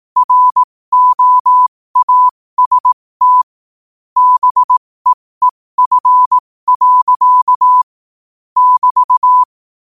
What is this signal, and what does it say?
Morse code, heard as a keyed tone, about 18 wpm, spaced as for 15 wpm, 1000 Hz. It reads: ROAST BEEF. <BT>
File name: qotdfiles / 15wpm